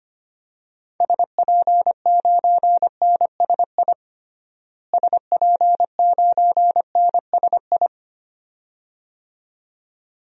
Morse code, heard as a keyed tone, about 25 wpm, 700 Hz. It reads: SP9NSI SP9NSI
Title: DXCC-100 Call Signs - 6 Characters - Repeated 2x - 25wpm